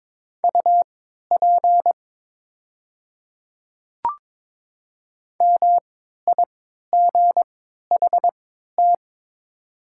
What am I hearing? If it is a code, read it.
Morse code, heard as a keyed tone, about 22 wpm, spaced as for 12 wpm, 700 Hz. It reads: UP MIGHT
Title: 3x Character Spacing - Top 300 Words - 22wpm